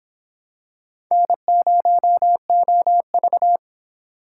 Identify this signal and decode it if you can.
Morse code, heard as a keyed tone, about 26 wpm, 700 Hz. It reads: N0OV